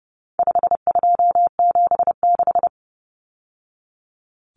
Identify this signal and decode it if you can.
Morse code, heard as a keyed tone, about 30 wpm, 700 Hz. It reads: 5276
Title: Numbers - 4-Digit 30wpm